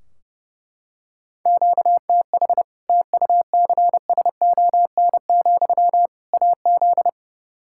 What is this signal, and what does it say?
Morse code, heard as a keyed tone, about 30 wpm, 700 Hz. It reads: QTH TUCSON, AZ
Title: Practice QSO - Exchange 1 - Beginning - 30wpm